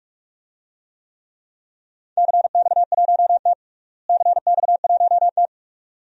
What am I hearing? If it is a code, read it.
Morse code, heard as a keyed tone, about 41 wpm, 700 Hz. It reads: CX1T CX1T
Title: DXCC-100 Call Signs - 4 Characters - Repeated 2x - 45wpm